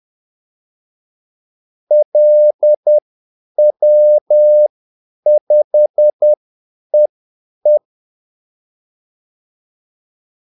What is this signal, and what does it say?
Morse code, heard as a keyed tone, about 10 wpm, spaced as for 8 wpm, 600 Hz. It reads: LW5EE
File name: Callsigns-10-8wpm